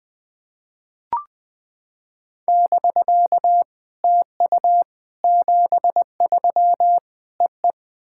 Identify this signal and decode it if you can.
Morse code, heard as a keyed tone, about 20 wpm, 700 Hz. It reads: <BK> TU 73 EE
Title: SOTA QSO Practice 20wpm